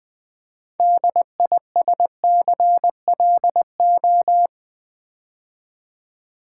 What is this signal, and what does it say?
Morse code, heard as a keyed tone, about 20 wpm, 700 Hz. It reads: DISCLO
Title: Build A Word - 2 Letters At A Time - 8-Letter Words - 20wpm